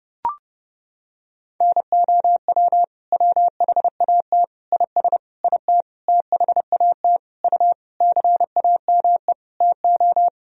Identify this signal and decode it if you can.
Morse code, heard as a keyed tone, about 30 wpm, 700 Hz. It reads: NOW WHAT IS IT THAT U CAME TO
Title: Sentences from Top 200 Words and Abbreviations 30wpm